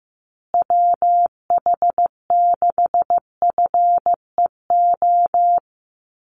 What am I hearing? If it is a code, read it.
Morse code, heard as a keyed tone, about 15 wpm, 700 Hz. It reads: WH6FEO
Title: USA Call Signs Repeated 3x 15wpm